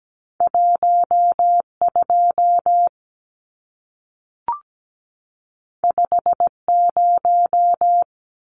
Morse code, heard as a keyed tone, about 17 wpm, 700 Hz. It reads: 12 50
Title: Numbers - 2-Digit 17wpm